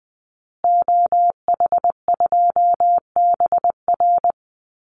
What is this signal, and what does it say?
Morse code, heard as a keyed tone, about 20 wpm, 700 Hz. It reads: OH2BR
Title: Call Signs Repeated 3x - 20wpm